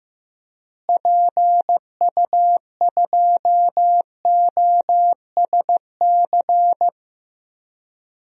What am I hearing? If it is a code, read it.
Morse code, heard as a keyed tone, about 15 wpm, 700 Hz. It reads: PU2OSC
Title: DXCC-100 Call Signs - 6 Characters - Repeated 2x - 15wpm